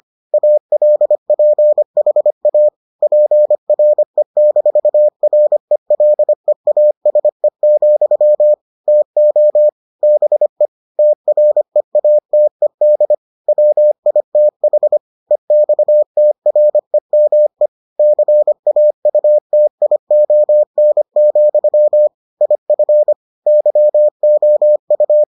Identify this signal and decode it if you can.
Morse code, heard as a keyed tone, about 25 wpm, 600 Hz. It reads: ALPHA PRE-RELEASE, TO BE TREATED WITH EXTREME CAUTION, IF YOU